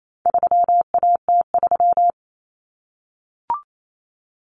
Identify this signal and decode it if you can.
Morse code, heard as a keyed tone, about 28 wpm, 700 Hz. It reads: 3AT3
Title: Common Cut Numbers - 4-Digit 28wpm